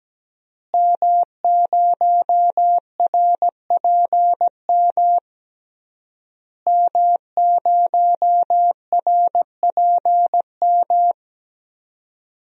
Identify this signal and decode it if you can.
Morse code, heard as a keyed tone, about 17 wpm, 700 Hz. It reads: M0RPM M0RPM
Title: DXCC-100 Call Signs - 5 Characters - Repeated 3x - 17wpm